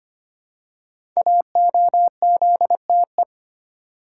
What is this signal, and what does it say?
Morse code, heard as a keyed tone, about 25 wpm, 700 Hz. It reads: AOZTE
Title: Code Groups - 5 Letters - 25wpm